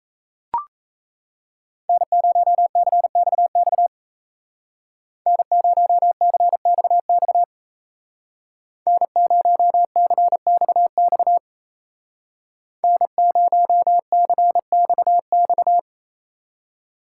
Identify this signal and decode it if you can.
Morse code, heard as a keyed tone, about 34 wpm, 700 Hz. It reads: N0CXX N0CXX N0CXX N0CXX
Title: Speed Racing - USA 1x3 Call Signs 28wpm